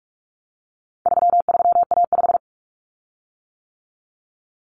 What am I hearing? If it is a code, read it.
Morse code, heard as a keyed tone, about 44 wpm, 700 Hz. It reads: 33A5
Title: Common Cut Numbers - 4-Digit 45wpm